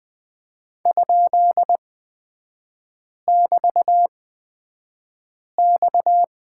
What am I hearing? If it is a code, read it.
Morse code, heard as a keyed tone, about 20 wpm, spaced as for 5 wpm, 690 Hz. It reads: ?<BT>X
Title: 229 Lesson 37e - Letter x - Three Characters 20wpm